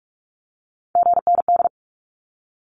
Morse code, heard as a keyed tone, about 44 wpm, 700 Hz. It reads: GND
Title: Abbreviations 45wpm